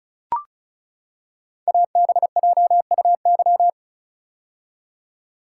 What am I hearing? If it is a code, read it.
Morse code, heard as a keyed tone, about 36 wpm, 700 Hz. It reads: ABJUY
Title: Code Groups - 5 Letters - 35wpm